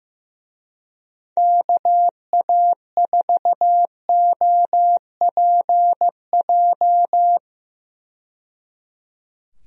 Morse code, heard as a keyed tone, about 15 wpm, 700 Hz. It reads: KA4OPJ
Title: USA Call Signs - 6 Characters - 15wpm